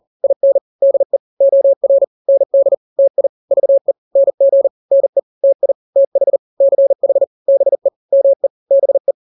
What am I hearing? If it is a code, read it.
Morse code, heard as a keyed tone, about 40 wpm, 550 Hz. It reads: IN DE OR ND TI VE NG NE TI TH CH BE ME BE
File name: n-grams / 2-grams 40 35 WPM